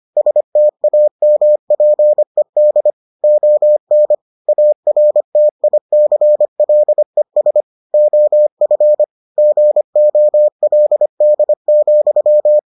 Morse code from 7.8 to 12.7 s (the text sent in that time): OF GOLD,